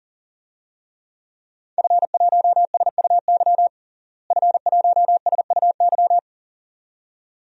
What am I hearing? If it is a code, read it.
Morse code, heard as a keyed tone, about 40 wpm, 700 Hz. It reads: F1SUY F1SUY
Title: DXCC-100 Call Signs - 5 Characters - Repeated 2x - 40wpm